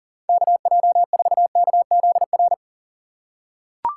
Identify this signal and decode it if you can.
Morse code, heard as a keyed tone, about 40 wpm, 700 Hz. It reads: KJ4KZR
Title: USA Call Signs - 6 Characters - 2x Character Spacing - 40wpm